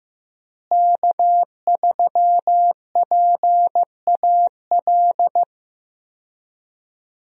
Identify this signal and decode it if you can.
Morse code, heard as a keyed tone, about 15 wpm, 700 Hz. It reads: K3PAL
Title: USA Call Signs - 5 Characters - Repeated 2x - 15wpm